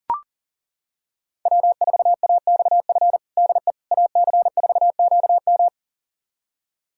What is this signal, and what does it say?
Morse code, heard as a keyed tone, about 37 wpm, 700 Hz. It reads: W4AXF DE AC4QM
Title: USA Callsign DE Callsign - 5 Characters - 40wpm